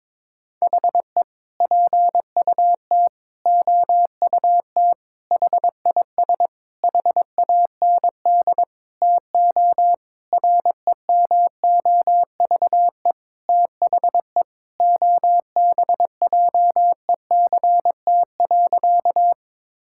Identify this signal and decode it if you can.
Morse code, heard as a keyed tone, about 21 wpm, 700 Hz. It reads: HE PUT OUT HIS HAND TO REMOVE THE OBJECT.
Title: Sentences from Top 1000 Words - Review Part 5 - 22wpm